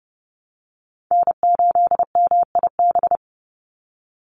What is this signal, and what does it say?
Morse code, heard as a keyed tone, about 30 wpm, 700 Hz. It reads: N8MIB
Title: USA Call Signs 30wpm